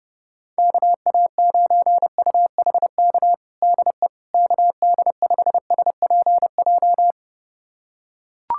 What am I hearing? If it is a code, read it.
Morse code, heard as a keyed tone, about 30 wpm, 700 Hz. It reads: KA9UHK DE KD5SPJ
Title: USA Callsign DE Callsign - 6 Characters - 30wpm